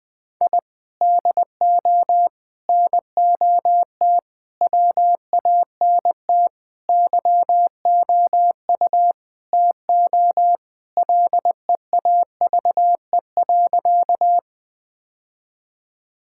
Morse code, heard as a keyed tone, about 20 wpm, 700 Hz. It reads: I DO NOT WANT YOU TO LEAVE.
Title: Sentences with 5-Word Phrases - Spoken Beginning - 20wpm